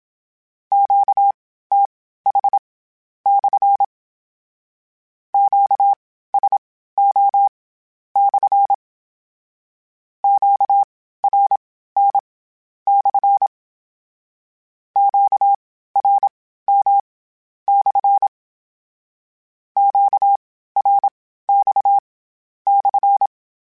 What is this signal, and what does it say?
Morse code, heard as a keyed tone, about 27 wpm, spaced as for 13 wpm, 800 Hz. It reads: QTH/ QSO/ QRN/ QRM/ QRX/
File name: Q Calls / Q Calls-mp3s / Q Calls SPSL1 @25